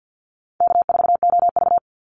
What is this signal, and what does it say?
Morse code, heard as a keyed tone, about 48 wpm, 700 Hz. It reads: K4OV
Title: Call Signs Repeated 2x - 50wpm